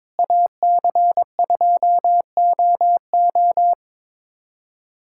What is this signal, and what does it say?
Morse code, heard as a keyed tone, about 22 wpm, 700 Hz. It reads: AC2OO
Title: USA Callsign DE Callsign - 5 Characters - 22wpm